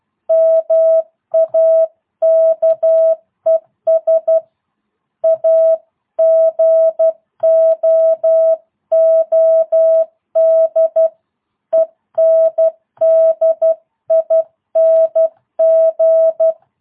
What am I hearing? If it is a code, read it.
Morse code, heard as a keyed tone, about 12 wpm, 650 Hz. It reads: MAKES A GOOD ENDING
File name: jh8cbh / 2022cw